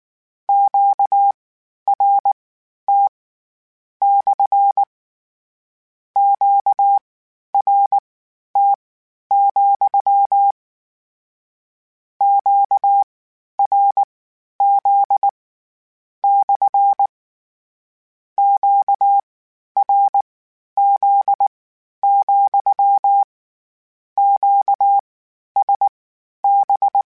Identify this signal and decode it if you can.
Morse code, heard as a keyed tone, about 19 wpm, spaced as for 11 wpm, 800 Hz. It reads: QRT / QRT, QRZ / QRZ, QSB
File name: Q Calls / Q Calls-mp3s / Q Calls SPSL2 @18